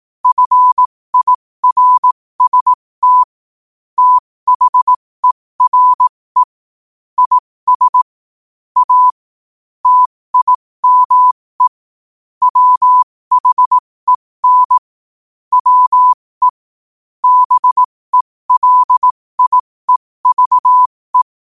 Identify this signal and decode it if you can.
Morse code, heard as a keyed tone, about 18 wpm, spaced as for 15 wpm, 1000 Hz. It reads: FIRST THERE IS A TIME WHEN WE BELIEVE
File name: qotdfiles / 15wpm